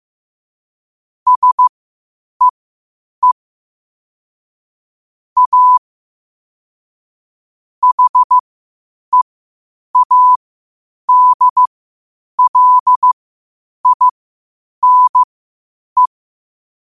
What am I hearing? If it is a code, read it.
Morse code, heard as a keyed tone, about 15 wpm, spaced as for 8 wpm, 1000 Hz. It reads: SEE A HEADLINE
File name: qotdfiles / 8wpm